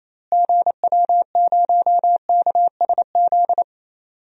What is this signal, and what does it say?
Morse code, heard as a keyed tone, about 28 wpm, 700 Hz. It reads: GW0KSZ
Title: DXCC-100 Call Signs - 6 Characters - Repeated 3x - 28wpm